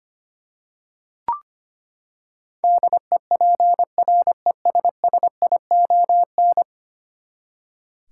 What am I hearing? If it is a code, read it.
Morse code, heard as a keyed tone, about 25 wpm, 700 Hz. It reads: DEPRESSION
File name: Ten Letter Words - 25wpm